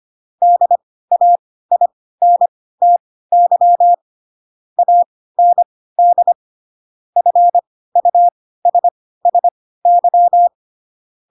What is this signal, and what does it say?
Morse code, heard as a keyed tone, about 25 wpm, spaced as for 16 wpm, 700 Hz. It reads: DAINTY AND FUSSY
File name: Pinocchio / Pinocchio ch7 10 25 ik1hge